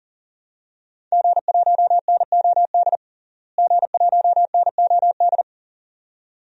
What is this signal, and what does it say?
Morse code, heard as a keyed tone, about 40 wpm, 700 Hz. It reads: G1NOD G1NOD
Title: DXCC-100 Call Signs - 5 Characters - Repeated 2x - 40wpm